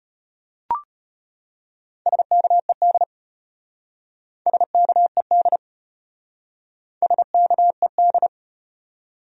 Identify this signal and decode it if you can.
Morse code, heard as a keyed tone, about 34 wpm, 700 Hz. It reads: SKED SKED SKED